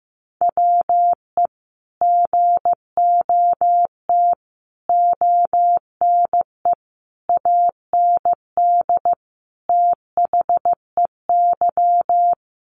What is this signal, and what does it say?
Morse code, heard as a keyed tone, about 15 wpm, 700 Hz. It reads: WE GOT ONE AND THEY
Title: Sentences from Top 300 Words 15wpm